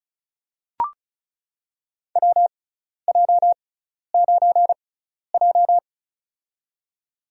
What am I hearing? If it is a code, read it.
Morse code, heard as a keyed tone, about 36 wpm, spaced as for 12 wpm, 700 Hz. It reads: WJ9J
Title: USA Call Signs - 4 Characters - 6x Character Spacing - 35wpm